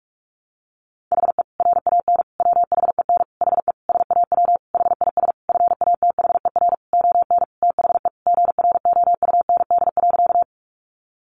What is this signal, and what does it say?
Morse code, heard as a keyed tone, about 44 wpm, 700 Hz. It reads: HE RAN WHEN HE SAW HIS FATHER ON THE GROUND.
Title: Sentences from Top 500 Words 45wpm